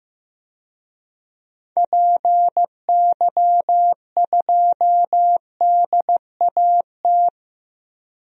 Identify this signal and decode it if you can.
Morse code, heard as a keyed tone, about 15 wpm, 700 Hz. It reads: PY2DAT
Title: DXCC-100 Call Signs - 6 Characters - Repeated 3x - 15wpm